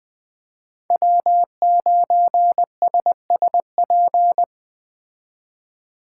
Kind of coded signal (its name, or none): Morse code